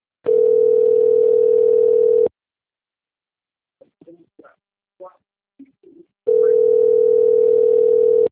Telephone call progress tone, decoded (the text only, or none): ringback tone